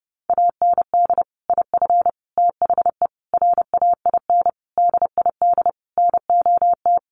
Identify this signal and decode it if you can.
Morse code, heard as a keyed tone, about 30 wpm, 700 Hz. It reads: AND IF THE RAIN DID NOT
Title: Sentences from Top 1300 Words 30wpm